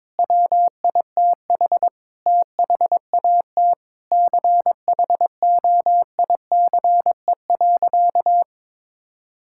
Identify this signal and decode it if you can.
Morse code, heard as a keyed tone, about 22 wpm, 700 Hz. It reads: WITH THAT CHOICE.
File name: Sentences with 7-Word Phrases - 22wpm